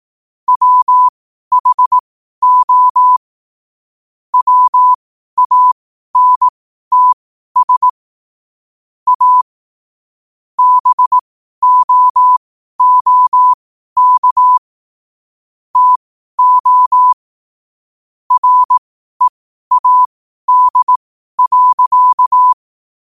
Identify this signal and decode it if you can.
Morse code, heard as a keyed tone, about 18 wpm, spaced as for 12 wpm, 1000 Hz. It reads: WHO WANTS A BOOK TO READ.
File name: qotdfiles / 12wpm